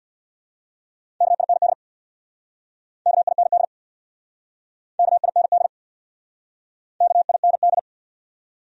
Morse code, heard as a keyed tone, about 69 wpm, 700 Hz. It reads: KIND KIND KIND KIND